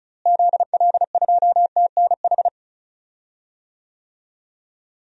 Morse code, heard as a keyed tone, about 34 wpm, 700 Hz. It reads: ZL2TNH
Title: DXCC-100 Call Signs - 6 Characters - Repeated 2x - 35wpm